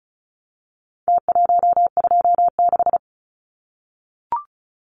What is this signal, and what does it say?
Morse code, heard as a keyed tone, about 34 wpm, 700 Hz. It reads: T126
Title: Numbers - Contest Style - 4-Digit 35wpm